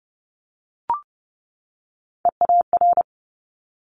Morse code, heard as a keyed tone, about 30 wpm, 700 Hz. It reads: EAR
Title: Top Three Letter Combinations in Words 30wpm